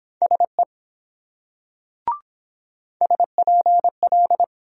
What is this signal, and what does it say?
Morse code, heard as a keyed tone, about 23 wpm, 700 Hz. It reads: SE SPL